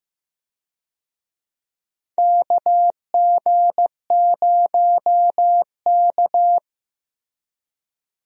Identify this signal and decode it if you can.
Morse code, heard as a keyed tone, about 15 wpm, 700 Hz. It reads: KG0K
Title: USA Call Signs - 4 Characters - Repeated 2x - 15wpm